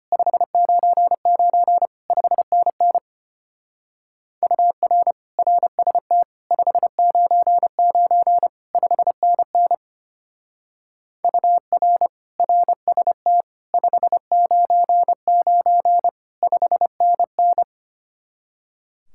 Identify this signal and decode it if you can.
Morse code, heard as a keyed tone, about 31 wpm, 700 Hz. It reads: 599 5NN UR RST 599 5NN UR RST 599 5NN
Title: Speed-Racing Common QSO Phrases - 25wpm